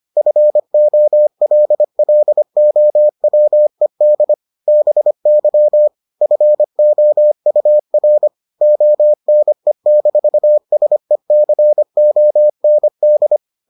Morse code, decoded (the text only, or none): FOLLOWED BY FOUR ONE-SECOND